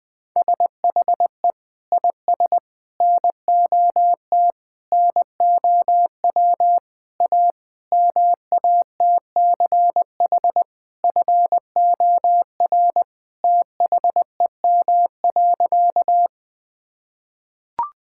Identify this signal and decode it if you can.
Morse code, heard as a keyed tone, about 20 wpm, 700 Hz. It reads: SHE IS NOT NOW A MATCH FOR THEM.
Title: Sentences from Top 1600 Words 20wpm